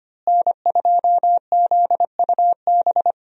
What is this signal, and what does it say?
Morse code, heard as a keyed tone, about 25 wpm, 700 Hz. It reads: N2ZUB
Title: USA Call Signs - 5 Characters - Repeated 3x - 25wpm